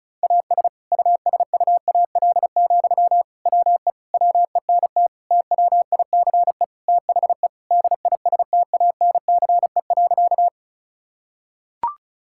Morse code, heard as a keyed tone, about 34 wpm, 700 Hz. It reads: AS USUAL, WE WENT TWICE THE DISTANCE.
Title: Sentences from Top 1600 Words 35wpm